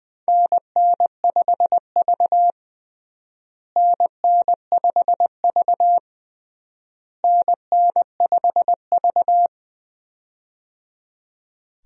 Morse code, heard as a keyed tone, about 20 wpm, 690 Hz. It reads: NN5V NN5V NN5V
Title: USA Call Signs - 4 Characters - Repeated 3x - 20wpm